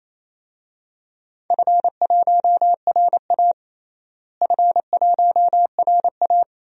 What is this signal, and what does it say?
Morse code, heard as a keyed tone, about 27 wpm, 700 Hz. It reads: F1RA F1RA
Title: DXCC-100 Call Signs - 4 Characters - Repeated 2x - 28wpm